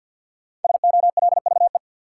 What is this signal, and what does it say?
Morse code, heard as a keyed tone, about 52 wpm, 700 Hz. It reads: SOLVE
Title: Sentences from Top 1000 Words - New Word Drill - 50wpm